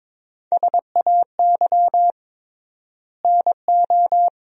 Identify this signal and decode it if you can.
Morse code, heard as a keyed tone, about 21 wpm, 700 Hz. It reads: SAY NO
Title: 3x Word-Spacing - Sentences from Top 200 Words - 22wpm